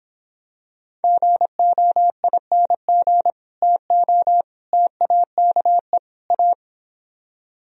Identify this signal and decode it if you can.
Morse code, heard as a keyed tone, about 24 wpm, 700 Hz. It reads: GOING TO TAKE A